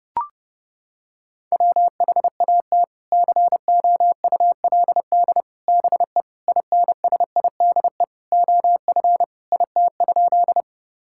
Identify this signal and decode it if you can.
Morse code, heard as a keyed tone, about 32 wpm, 700 Hz. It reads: WHAT COULD BE INSIDE OF IT?
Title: Sentences from Top 1000 Words - Review Part 6 - 30wpm